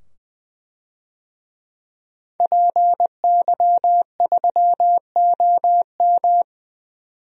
Morse code, heard as a keyed tone, about 20 wpm, 700 Hz. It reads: PY3OM
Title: DXCC-100 Call Signs - 5 Characters - 20wpm